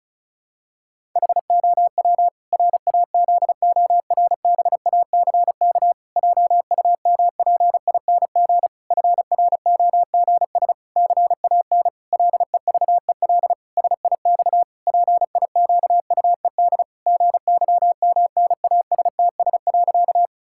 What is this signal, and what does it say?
Morse code, heard as a keyed tone, about 36 wpm, 700 Hz. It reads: HOW RAZORBACK JUMPING FROGS CAN LEVEL SIX PIQUED GYMNASTS.
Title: Pangram Challenge v2 - 35wpm